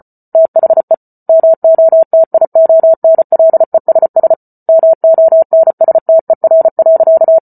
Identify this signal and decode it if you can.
Morse code, heard as a keyed tone, about 34 wpm, 650 Hz. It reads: THE MOTIONLESS MONSTER.
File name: Book Two / 08-Chapter 8 - 34 WPM